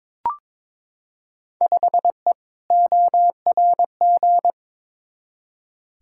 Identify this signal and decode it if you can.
Morse code, heard as a keyed tone, about 22 wpm, 700 Hz. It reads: 5E ORG
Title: Field Day Exchanges Encore - 22wpm